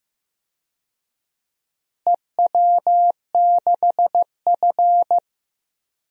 Morse code, heard as a keyed tone, about 15 wpm, 700 Hz. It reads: EW6F
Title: DXCC-100 Call Signs - 4 Characters - Repeated 2x - 15wpm